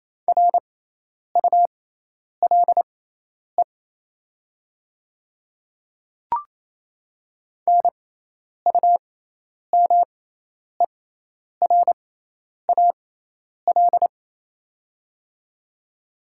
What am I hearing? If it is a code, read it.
Morse code, heard as a keyed tone, about 28 wpm, spaced as for 10 wpm, 690 Hz. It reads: RULE NUMERAL
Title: 6x Character Spacing - Top 500 Words - 28wpm